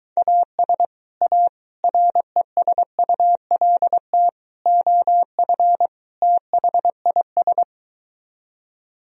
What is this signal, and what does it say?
Morse code, heard as a keyed tone, about 24 wpm, 700 Hz. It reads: AS A RESULT OF THIS